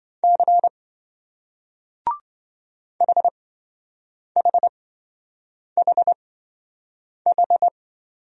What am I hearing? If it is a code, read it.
Morse code, heard as a keyed tone, about 27 wpm, spaced as for 7 wpm, 700 Hz. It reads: C HHHH